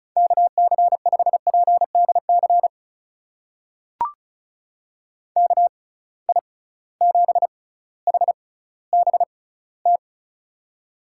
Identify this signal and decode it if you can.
Morse code, heard as a keyed tone, about 36 wpm, spaced as for 18 wpm, 700 Hz. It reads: KC5PDC KI7HBT
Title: USA Call Signs - 6 Characters - 6x Character Spacing - 35wpm